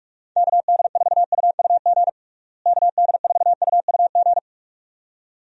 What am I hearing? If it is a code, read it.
Morse code, heard as a keyed tone, about 46 wpm, 700 Hz. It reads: KD4UUG KD4UUG
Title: USA Call Signs - 6 Characters - Repeated 2x - 45wpm